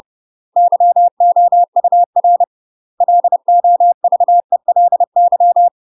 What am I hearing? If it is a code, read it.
Morse code, heard as a keyed tone, about 30 wpm, 700 Hz. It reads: YOUR LOVELY
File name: UK MASTER QSOs TRAINING / 30WPM / UK MASTER QSOs 0014